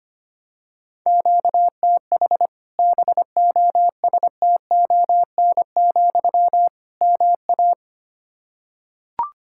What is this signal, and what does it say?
Morse code, heard as a keyed tone, about 25 wpm, 700 Hz. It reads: QTH BOSTON, MA
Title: Practice QSO - Exchange 1 - Beginning - 25wpm